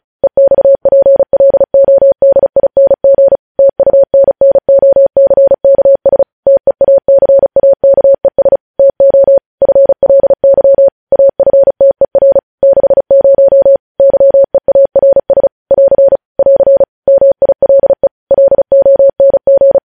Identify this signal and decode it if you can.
Morse code, heard as a keyed tone, about 36 wpm, 550 Hz. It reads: EXPLODING TUNNOCKS TEACAKES TO FLY AFTER 60 YEARS <AR> <AR> MILE LONG
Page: news headlines in morse code at 35 wpm